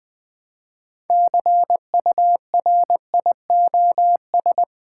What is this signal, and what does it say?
Morse code, heard as a keyed tone, about 20 wpm, 700 Hz. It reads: CURIOS
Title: Build A Word - 3 Letters At A Time - 9-Letter Words - 20wpm